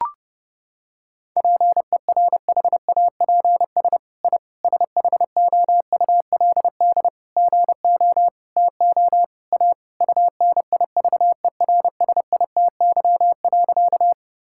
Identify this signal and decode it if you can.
Morse code, heard as a keyed tone, about 30 wpm, 700 Hz. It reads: PERHAPS I SHOULD GO TO A UNIVERSITY.
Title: Sentences from Top 1000 Words - Review Part 5 - 30wpm